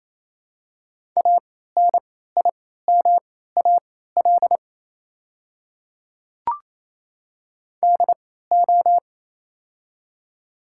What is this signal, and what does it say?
Morse code, heard as a keyed tone, about 28 wpm, spaced as for 16 wpm, 690 Hz. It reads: ANIMAL DO
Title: 3x Character Spacing - Top 200 Words - 28wpm